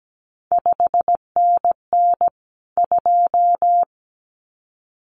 Morse code, heard as a keyed tone, about 17 wpm, 700 Hz. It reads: 5NN 2
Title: CQ WW DX Contest Exchange Encore - 17wpm